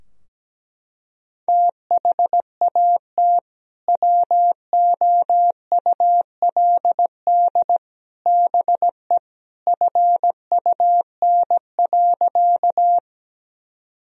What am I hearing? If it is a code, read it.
Morse code, heard as a keyed tone, about 17 wpm, 700 Hz. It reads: THAT WOULD BE FUN.
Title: Sentences from Top 1000 Words - Review Part 4 - 17wpm